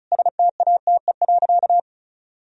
Morse code, heard as a keyed tone, about 34 wpm, 700 Hz. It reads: STATE.
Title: Sentences from Top 1000 Words - Review Part 2 - 35wpm